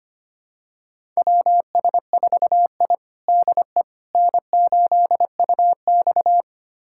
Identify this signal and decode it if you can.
Morse code, heard as a keyed tone, about 25 wpm, 700 Hz. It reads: WS4I DE N8UX
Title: USA Callsign DE Callsign - 4 Characters - 25wpm